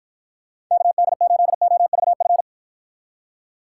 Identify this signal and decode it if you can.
Morse code, heard as a keyed tone, about 53 wpm, 700 Hz. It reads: KD8OVF